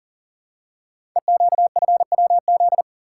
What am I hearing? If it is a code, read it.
Morse code, heard as a keyed tone, about 38 wpm, 700 Hz. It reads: EQFWZ
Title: Code Groups - 5 Letter:Numbers - 40wpm